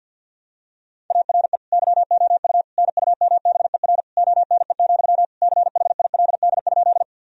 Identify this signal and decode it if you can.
Morse code, heard as a keyed tone, about 49 wpm, 700 Hz. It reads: ARE YOU NUMBER ONE, CHILD?
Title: Sentences from Top 1000 Words - Review Part 1 - 50wpm